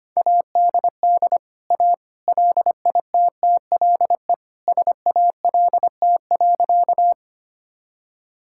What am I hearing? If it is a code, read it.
Morse code, heard as a keyed tone, about 25 wpm, 700 Hz. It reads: ADD A LITTLE SALT.
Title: Sentences from Top 1000 Words - Review Part 5 - 25wpm